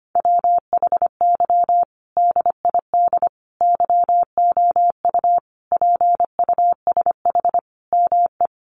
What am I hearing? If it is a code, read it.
Morse code, heard as a keyed tone, about 25 wpm, 700 Hz. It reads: WHY DID YOU PUSH ME
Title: Sentences from Top 1300 Words 25wpm